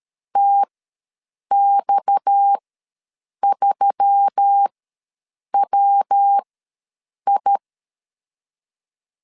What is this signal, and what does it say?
Morse code, heard as a keyed tone, about 13 wpm, spaced as for 7 wpm, 800 Hz. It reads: TX3WI
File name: LEZIONI / 6 PKX0 / 6 PKX ws 12wpm 3106